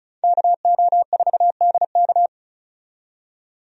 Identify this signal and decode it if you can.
Morse code, heard as a keyed tone, about 36 wpm, 700 Hz. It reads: KO4DK
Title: USA Call Signs - 5 Characters - 35wpm